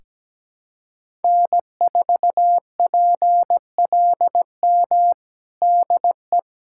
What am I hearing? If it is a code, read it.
Morse code, heard as a keyed tone, about 17 wpm, 700 Hz. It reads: N4PLM DE
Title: USA Callsign DE Callsign 17wpm